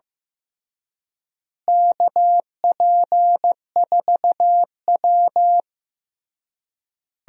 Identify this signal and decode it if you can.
Morse code, heard as a keyed tone, about 16 wpm, 700 Hz. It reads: KP4W
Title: DXCC-100 Call Signs - 4 Characters - Repeated 3x - 15wpm